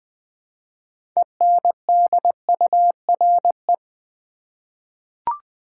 Morse code, heard as a keyed tone, about 20 wpm, 700 Hz. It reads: ENDURE
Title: 6 Letter Words - First 3 Letters Given - 20wpm